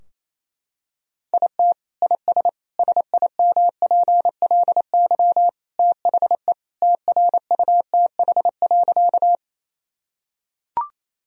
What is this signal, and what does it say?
Morse code, heard as a keyed tone, about 28 wpm, 700 Hz. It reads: IT IS SIMPLY THE TRUTH.
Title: Sentences from Top 1000 Words - Review Part 4 - 28wpm